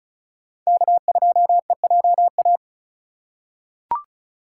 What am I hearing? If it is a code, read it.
Morse code, heard as a keyed tone, about 36 wpm, 700 Hz. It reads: K2EJA
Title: USA Call Signs - 5 Characters - 3x Character Spacing - 35wpm